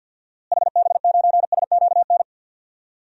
Morse code, heard as a keyed tone, about 47 wpm, 700 Hz. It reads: HB9SQN
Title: DXCC-100 Call Signs - 6 Characters - Repeated 2x - 50wpm